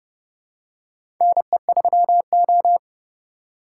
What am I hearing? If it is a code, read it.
Morse code, heard as a keyed tone, about 29 wpm, 700 Hz. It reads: NE3O
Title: USA Call Signs - 4 Characters - 2x Character Spacing - 30wpm